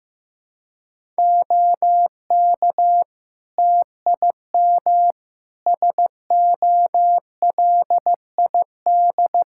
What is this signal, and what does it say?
Morse code, heard as a keyed tone, about 15 wpm, 700 Hz. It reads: OK TIM SOLID
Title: Practice QSO - Exchange 3 - More Info - 15wpm